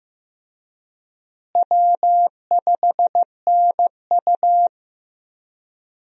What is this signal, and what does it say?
Morse code, heard as a keyed tone, about 15 wpm, 700 Hz. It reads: W5NU
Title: USA Call Signs - 4 Characters - Repeated 2x - 15wpm